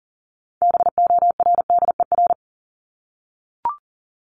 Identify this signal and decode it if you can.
Morse code, heard as a keyed tone, about 40 wpm, 700 Hz. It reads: BORDER
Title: Top 1751-2000 Words 40wpm